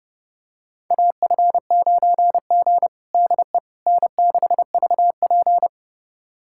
Morse code, heard as a keyed tone, about 29 wpm, 700 Hz. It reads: AF9G DE N6VP
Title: USA Callsign DE Callsign - 4 Characters - 30wpm